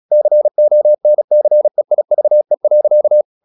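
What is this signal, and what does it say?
Morse code, heard as a keyed tone, about 36 wpm, 600 Hz. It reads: CONCEIVE.
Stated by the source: Morse practice at 36 wpm, text of Ken Follett's novel Never, from CW med Gnister.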